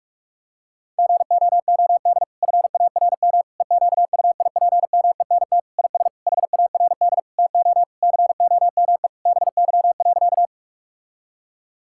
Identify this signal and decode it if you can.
Morse code, heard as a keyed tone, about 43 wpm, 700 Hz. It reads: GOOD FARM EQUIPMENT IS HARD TO COME BY.
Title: Sentences from Top 1000 Words - Review Part 5 - 45wpm